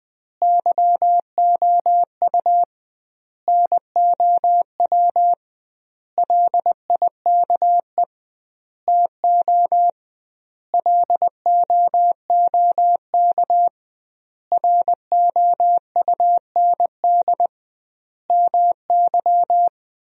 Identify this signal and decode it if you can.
Morse code, heard as a keyed tone, about 21 wpm, 700 Hz. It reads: YOU NOW LIKE TO LOOK ROUND MY
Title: 2x Word-Spacing - Sentences from Top 200 Words - 20wpm